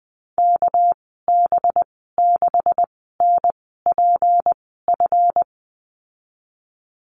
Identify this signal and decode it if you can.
Morse code, heard as a keyed tone, about 20 wpm, spaced as for 14 wpm, 700 Hz. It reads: KB6NPF
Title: 245 Lesson 39g - Letter z - Call Signs 20wpm